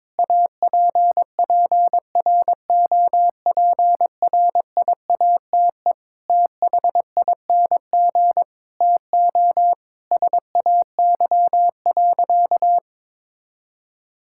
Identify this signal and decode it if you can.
Morse code, heard as a keyed tone, about 22 wpm, 700 Hz. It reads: APPROPRIATE THING TO SAY.
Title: Sentences from Top 1000 Words - Review Part 5 - 22wpm